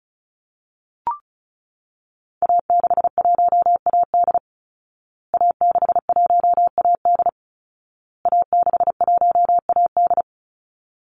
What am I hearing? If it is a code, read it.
Morse code, heard as a keyed tone, about 34 wpm, 700 Hz. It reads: A61AD A61AD A61AD
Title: Call Signs Repeated 2x - 35wpm